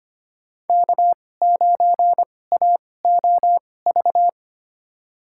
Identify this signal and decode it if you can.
Morse code, heard as a keyed tone, about 25 wpm, spaced as for 18 wpm, 700 Hz. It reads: K9AOV
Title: USA Call Signs - 5 Characters - 2x Character Spacing - 25wpm